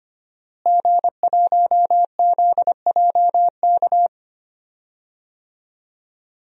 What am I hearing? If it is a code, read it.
Morse code, heard as a keyed tone, about 25 wpm, 700 Hz. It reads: G1ZJK
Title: DXCC-100 Call Signs - 5 Characters - Repeated 3x - 25wpm